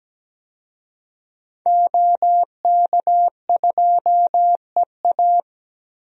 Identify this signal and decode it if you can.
Morse code, heard as a keyed tone, about 17 wpm, 700 Hz. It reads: OK2EA
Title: DXCC-100 Call Signs - 5 Characters - Repeated 2x - 17wpm